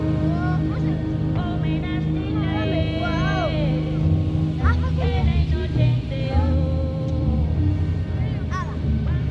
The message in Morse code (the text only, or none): none